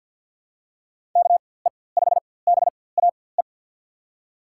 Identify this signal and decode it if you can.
Morse code, heard as a keyed tone, about 53 wpm, spaced as for 24 wpm, 700 Hz. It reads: KE5BAE
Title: USA Call Signs - 4x Character Spacing - 50wpm